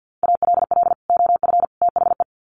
Morse code, heard as a keyed tone, about 50 wpm, 700 Hz. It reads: ALL OF THE
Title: Sets of 3 Words 50wpm